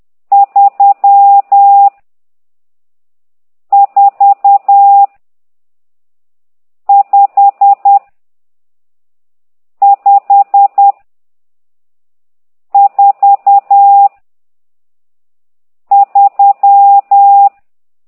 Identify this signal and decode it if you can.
Morse code, heard as a keyed tone, about 10 wpm, 800 Hz. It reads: 3 4 5 5 4 3
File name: ct2idw / exerc12